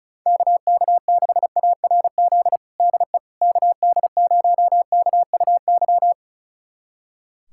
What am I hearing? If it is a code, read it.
Morse code, heard as a keyed tone, about 36 wpm, 700 Hz. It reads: KK6ARZ DE KD0KUY
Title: USA Callsign DE Callsign 35wpm